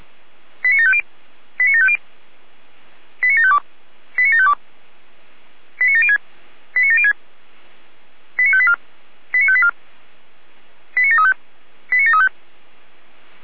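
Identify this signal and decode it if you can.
Five-tone selective call (ZVEI), heard as quick runs of five tones, tones of about 70 ms. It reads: 89750 89750 89752 89752 89706 89706 89584 89584 89735 89735
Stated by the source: ZVEI five-tone selective call received by SDR.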